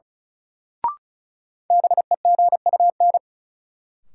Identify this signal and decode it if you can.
Morse code, heard as a keyed tone, about 34 wpm, 700 Hz. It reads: BEGUN
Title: Sentences from Top 1000 Words - Review Part 6 - New Word Drill - 35wpm